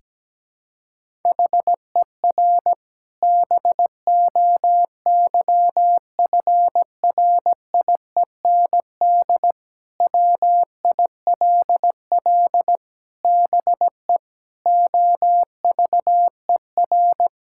Morse code, heard as a keyed tone, about 17 wpm, 700 Hz. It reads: HER BOYFRIEND WILL BE OVER
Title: Sentences from Top 1000 Words - Review Part 5 - 17wpm